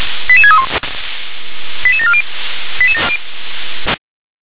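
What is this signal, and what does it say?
Five-tone selective call (ZVEI), heard as quick runs of five tones, tones of about 70 ms. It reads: 88631 88740
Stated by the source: Five-tone ZVEI selective call received on an SDR.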